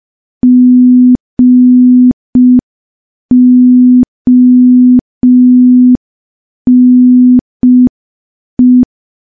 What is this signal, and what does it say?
Morse code, heard as a keyed tone, about 5 wpm, 250 Hz. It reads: GONE
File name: morse / morse-training-05WPM